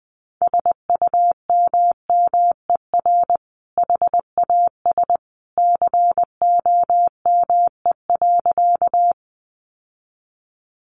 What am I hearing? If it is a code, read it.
Morse code, heard as a keyed tone, about 20 wpm, 700 Hz. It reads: SUMMER HAS COME.
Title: Sentences from Top 1100 Words 20wpm